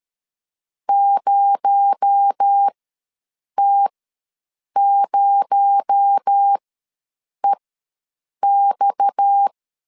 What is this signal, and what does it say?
Morse code, heard as a keyed tone, about 13 wpm, spaced as for 7 wpm, 800 Hz. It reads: 0T0EX
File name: LEZIONI / 6 PKX0 / 6 PKX0 ws 12wpm 3448